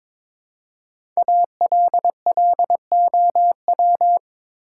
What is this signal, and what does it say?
Morse code, heard as a keyed tone, about 23 wpm, 700 Hz. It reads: ALLOW
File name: 5 Letter Words - First 3 Letters Given - 22wpm